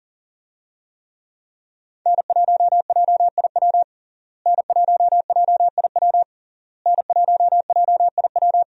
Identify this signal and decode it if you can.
Morse code, heard as a keyed tone, about 40 wpm, 700 Hz. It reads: N1JIW N1JIW N1JIW
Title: USA Call Signs - 5 Characters - Repeated 3x - 40wpm